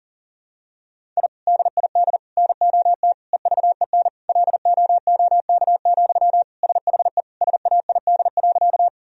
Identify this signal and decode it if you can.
Morse code, heard as a keyed tone, about 40 wpm, 700 Hz. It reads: I DID NOT EVEN LOOK, SHE SAID.
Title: Sentences from Top 1000 Words - Review Part 1 - 40wpm